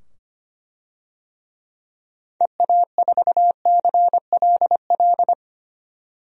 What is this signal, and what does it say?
Morse code, heard as a keyed tone, about 25 wpm, 700 Hz. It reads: EA4CLL
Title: DXCC-100 Call Signs - 6 Characters - 25wpm